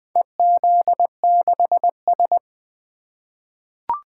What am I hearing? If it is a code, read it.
Morse code, heard as a keyed tone, about 20 wpm, 700 Hz. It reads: EZ6S
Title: Four Letter-Number Combinations - 20wpm